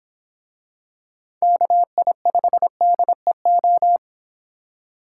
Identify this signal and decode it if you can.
Morse code, heard as a keyed tone, about 25 wpm, 700 Hz. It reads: KI5DEO